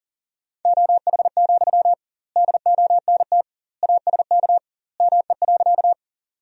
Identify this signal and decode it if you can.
Morse code, heard as a keyed tone, about 40 wpm, 700 Hz. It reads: OH, DONT ASK ME.
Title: 2x Word-Spacing - Sentences from Top 500 Words - 40wpm